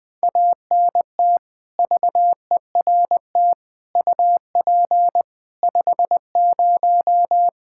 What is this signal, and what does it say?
Morse code, heard as a keyed tone, about 20 wpm, 700 Hz. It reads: ANT VERT UP 50
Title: Practice QSO - Exchange 2 - Info - 20wpm